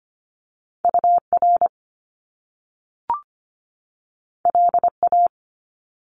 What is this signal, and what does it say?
Morse code, heard as a keyed tone, about 25 wpm, 700 Hz. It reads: UR LA
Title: Top Two Letter Combinations in Words 25wpm